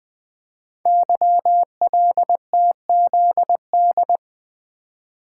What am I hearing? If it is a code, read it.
Morse code, heard as a keyed tone, about 20 wpm, 700 Hz. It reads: YLTZD
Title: Code Groups - 5 Letter:Numbers - 20wpm